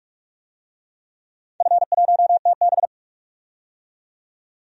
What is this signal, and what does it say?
Morse code, heard as a keyed tone, about 42 wpm, 700 Hz. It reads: F1TB
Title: DXCC-100 Call Signs - 4 Characters - 45wpm